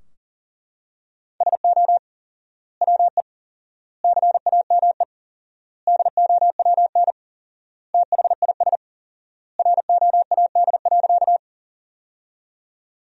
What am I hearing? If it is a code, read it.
Morse code, heard as a keyed tone, about 40 wpm, 700 Hz. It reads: SO WE CAME DOWN THIS ROAD.
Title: 4x Word-Spacing - Sentences from Top 400 Words - 40wpm